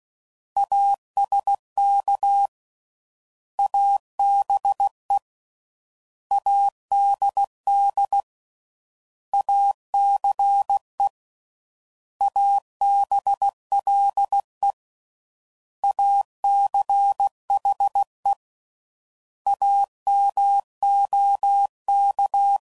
Morse code, heard as a keyed tone, about 16 wpm, 800 Hz. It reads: ASK ABE ADD ACE ABLE ACHE AMOK
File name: Prefixes / By First Letter / As